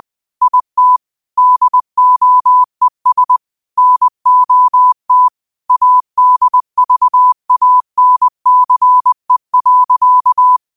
Morse code, heard as a keyed tone, about 20 wpm, 1000 Hz. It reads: IT DOES NOT ADVANCE.
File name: qotdfiles / 20wpm